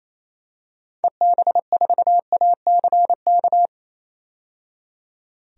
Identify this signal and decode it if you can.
Morse code, heard as a keyed tone, about 28 wpm, 700 Hz. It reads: EB4ACK
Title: DXCC-100 Call Signs - 6 Characters - 28wpm